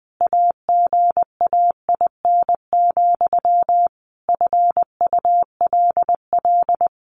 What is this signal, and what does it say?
Morse code, heard as a keyed tone, about 20 wpm, 700 Hz. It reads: AGAIN, FULL
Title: A Sound Of Thunder 20wpm